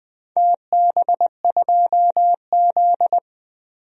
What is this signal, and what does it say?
Morse code, heard as a keyed tone, about 20 wpm, 700 Hz. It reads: TB2Z
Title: DXCC-100 Call Signs - 4 Characters - 20wpm